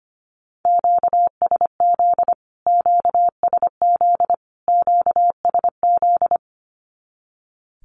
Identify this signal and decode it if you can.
Morse code, heard as a keyed tone, about 25 wpm, 700 Hz. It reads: QSZ QSZ QSZ
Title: Q-Codes Repeated 3x 25wpm